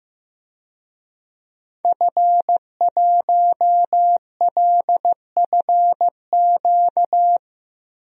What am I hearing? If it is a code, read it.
Morse code, heard as a keyed tone, about 15 wpm, 700 Hz. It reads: F1LFQ
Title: DXCC-100 Call Signs - 5 Characters - Repeated 3x - 15wpm